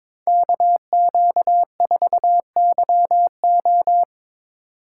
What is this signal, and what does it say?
Morse code, heard as a keyed tone, about 24 wpm, 700 Hz. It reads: KQ4YO
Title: Speed Racing - USA 2x2 Call Signs 22wpm